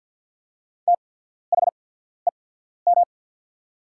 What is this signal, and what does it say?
Morse code, heard as a keyed tone, about 48 wpm, spaced as for 14 wpm, 690 Hz. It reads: THEM
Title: 8x Character Spacing - Top 300 Words - 50wpm